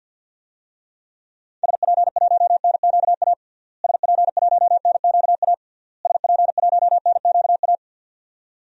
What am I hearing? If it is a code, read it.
Morse code, heard as a keyed tone, about 48 wpm, 700 Hz. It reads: SP1NQA SP1NQA SP1NQA
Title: DXCC-100 Call Signs - 6 Characters - Repeated 3x - 50wpm